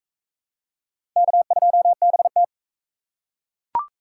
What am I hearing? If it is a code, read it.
Morse code, heard as a keyed tone, about 40 wpm, 700 Hz. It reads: K2BT